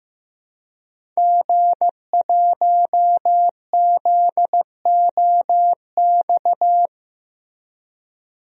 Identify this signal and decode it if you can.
Morse code, heard as a keyed tone, about 15 wpm, 700 Hz. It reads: G1ZOX
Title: DXCC-100 Call Signs - 5 Characters - Repeated 3x - 15wpm